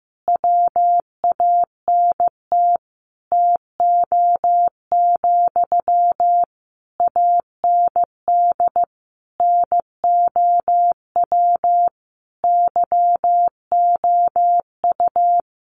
Morse code, heard as a keyed tone, about 15 wpm, 700 Hz. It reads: WANT TO, AND NOW YOU
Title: Sentences from Top 600 Words 15wpm